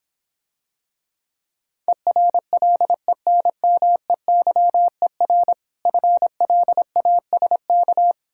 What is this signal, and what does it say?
Morse code, heard as a keyed tone, about 26 wpm, 700 Hz. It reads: ERLENMEYER FLASK